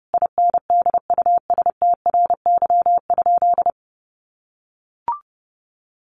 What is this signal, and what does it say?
Morse code, heard as a keyed tone, about 30 wpm, 700 Hz. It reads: INDUSTRY?
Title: Sentences from Top 1300 Words 30wpm